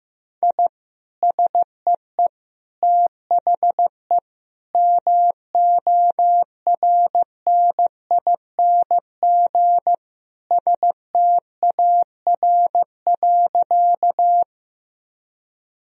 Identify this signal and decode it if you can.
Morse code, heard as a keyed tone, about 15 wpm, 700 Hz. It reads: I SEE THE MORNING STAR.
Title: Sentences from Top 1000 Words - Review Part 2 - 15wpm